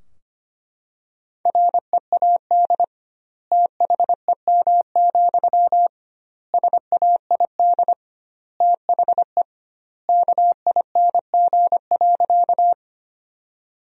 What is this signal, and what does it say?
Morse code, heard as a keyed tone, about 25 wpm, 700 Hz. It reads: READ THEM, SAID THE KING.
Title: 2x Word-Spacing - Sentences from Top 400 Words - 25wpm